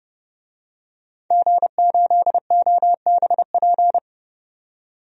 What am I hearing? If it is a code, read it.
Morse code, heard as a keyed tone, about 30 wpm, 700 Hz. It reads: G8OBP
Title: DXCC-100 Call Signs - 5 Characters - 30wpm